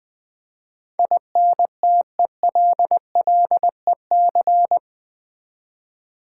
Morse code, heard as a keyed tone, about 20 wpm, 700 Hz. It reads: INTELLEC
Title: Build A Word - 4 Letters At A Time - 12-Letter Words - 20wpm